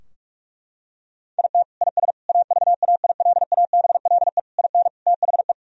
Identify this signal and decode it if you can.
Morse code, heard as a keyed tone, about 41 wpm, 700 Hz. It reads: IT IS AVAILABLE IN THE
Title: Sentences from Top 1000 Words - Review Part 3 - 45wpm